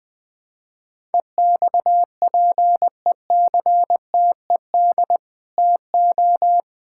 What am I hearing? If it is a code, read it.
Morse code, heard as a keyed tone, about 21 wpm, 700 Hz. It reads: EXPECTED TO
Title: Sets of 8 Words - 20wpm